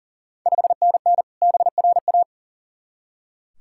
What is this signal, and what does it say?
Morse code, heard as a keyed tone, about 40 wpm, 690 Hz. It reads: 5NN BRA
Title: Ontario QSO Party Exchange 40wpm